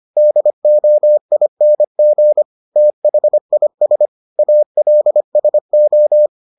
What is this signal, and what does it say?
Morse code, heard as a keyed tone, about 25 wpm, 600 Hz. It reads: DOING THIS ALSO